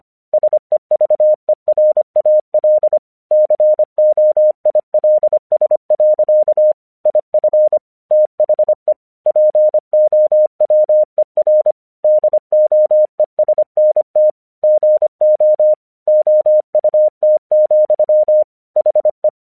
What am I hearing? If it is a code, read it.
Morse code, heard as a keyed tone, about 25 wpm, 610 Hz. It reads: SEVERAL COILS. IF THE POWER DOESNT GO OUT, HE
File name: 1 English 25wpm / AQC24